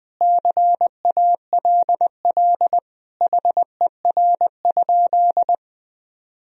Morse code, heard as a keyed tone, about 20 wpm, 700 Hz. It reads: CALL HER?
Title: Sentences from Top 100 Words and Abbreviations 20wpm